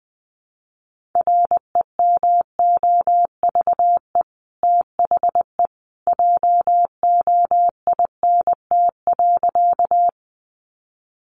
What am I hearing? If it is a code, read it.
Morse code, heard as a keyed tone, about 19 wpm, 700 Hz. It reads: REMOVE THE JOINT.
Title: Sentences from Top 1300 Words 20wpm